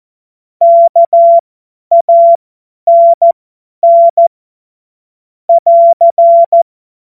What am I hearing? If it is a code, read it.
Morse code, heard as a keyed tone, about 14 wpm, spaced as for 10 wpm, 680 Hz. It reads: KANN <AR>
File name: Morsekurs / Witze / Witz 24. Januar 2023 50 F70